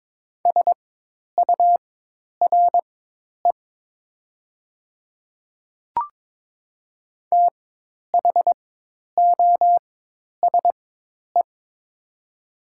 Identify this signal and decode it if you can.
Morse code, heard as a keyed tone, about 22 wpm, spaced as for 10 wpm, 700 Hz. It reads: SURE THOSE
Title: 4x Character Spacing - Top 300 Words - 22wpm